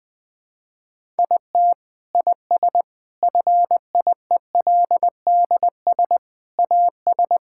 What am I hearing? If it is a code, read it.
Morse code, heard as a keyed tone, about 20 wpm, 700 Hz. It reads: IT IS FIELDS AS
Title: Sentences with 7-Word Phrases - 20wpm